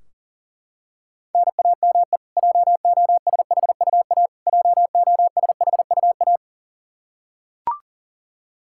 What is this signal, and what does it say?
Morse code, heard as a keyed tone, about 40 wpm, 700 Hz. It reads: NAME JOSHUA JOSHUA
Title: Practice QSO - Exchange 1 - Beginning - 40wpm